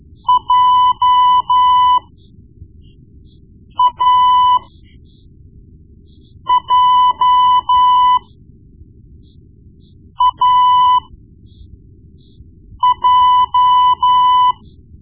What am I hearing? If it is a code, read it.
Morse code, heard as a keyed tone, about 12 wpm, spaced as for 4 wpm, 1000 Hz. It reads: JAJAJ
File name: Morse Lessons / Morse Lessons / Tape-1B